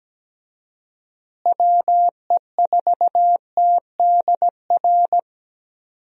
Morse code, heard as a keyed tone, about 17 wpm, 700 Hz. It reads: WE4TDR
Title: USA Call Signs - 6 Characters - Repeated 3x - 17wpm